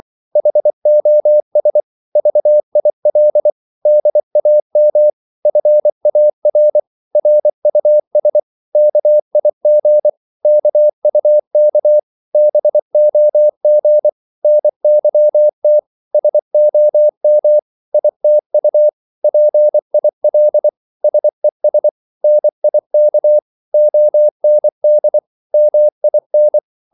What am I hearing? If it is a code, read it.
Morse code, heard as a keyed tone, about 24 wpm, 600 Hz. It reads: HOS VIL DAM FAR RUS KIG KUK BOG NYT SOM ITU PIL SES NIK OND MIN
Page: Korte danske ord 24wpm